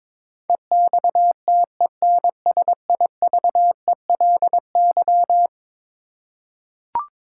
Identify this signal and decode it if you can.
Morse code, heard as a keyed tone, about 22 wpm, 700 Hz. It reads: EXTENSIVELY
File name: practice / 22wpm / Eleven Letter Words - 22wpm